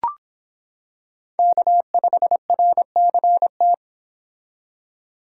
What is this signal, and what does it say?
Morse code, heard as a keyed tone, about 25 wpm, 700 Hz. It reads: K5RCT